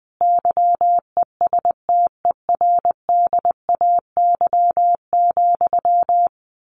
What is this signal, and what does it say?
Morse code, heard as a keyed tone, about 20 wpm, 700 Hz. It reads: YESTERDAY,
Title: A Sound Of Thunder 20wpm